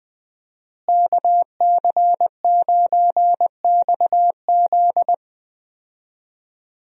Morse code, heard as a keyed tone, about 20 wpm, 700 Hz. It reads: KC9XZ
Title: USA Call Signs - 5 Characters - Repeated 2x - 20wpm